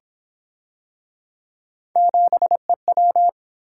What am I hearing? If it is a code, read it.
Morse code, heard as a keyed tone, about 31 wpm, 700 Hz. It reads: 7EW